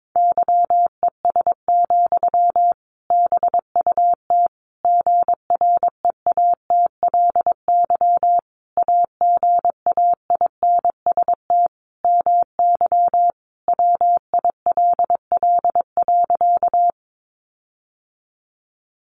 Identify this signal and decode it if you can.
Morse code, heard as a keyed tone, about 22 wpm, 700 Hz. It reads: YES, BUT GREATLY AGAINST MY WILL.
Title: Sentences from Top 1400 Words 22wpm